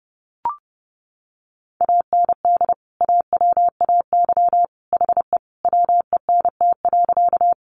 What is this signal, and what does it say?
Morse code, heard as a keyed tone, about 30 wpm, 700 Hz. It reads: AND AWAY HE WENT.
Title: Sentences from Top 600 Words 30wpm